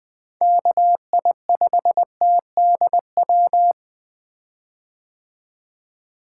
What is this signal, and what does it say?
Morse code, heard as a keyed tone, about 20 wpm, 700 Hz. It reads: KI5TDW
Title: 2023 POTA Activator Call Signs - 20wpm